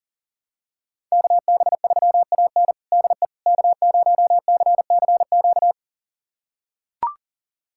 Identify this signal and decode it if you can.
Morse code, heard as a keyed tone, about 40 wpm, 700 Hz. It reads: KB3AN DE K0CCQ
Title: USA Callsign DE Callsign - 5 Characters - 40wpm